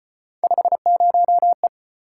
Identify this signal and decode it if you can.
Morse code, heard as a keyed tone, about 32 wpm, 700 Hz. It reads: 50E